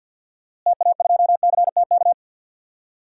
Morse code, heard as a keyed tone, about 51 wpm, 700 Hz. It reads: TA2CTK
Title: DXCC-100 Call Signs - 6 Characters - Repeated 3x - 50wpm